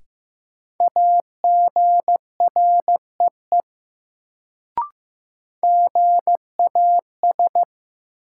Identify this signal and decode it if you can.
Morse code, heard as a keyed tone, about 15 wpm, 700 Hz. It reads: AGREE GAS
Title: Sentences from Top 1100 Words - New Word Drill Alt - 15wpm